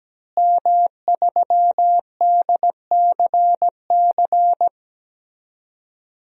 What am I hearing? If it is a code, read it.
Morse code, heard as a keyed tone, about 17 wpm, 700 Hz. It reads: M3DCC
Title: DXCC-100 Call Signs - 5 Characters - Repeated 3x - 17wpm